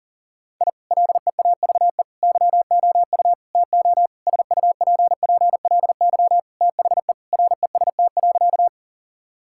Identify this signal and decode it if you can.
Morse code, heard as a keyed tone, about 40 wpm, 700 Hz. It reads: I LEAVE YOU TO SUPPLY THE REST.
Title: Sentences from Top 1000 Words - Review Part 5 - 40wpm